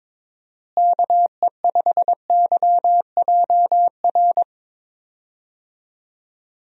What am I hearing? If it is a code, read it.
Morse code, heard as a keyed tone, about 22 wpm, 700 Hz. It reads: KE5YJR
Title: USA Call Signs - 6 Characters - Repeated 3x - 22wpm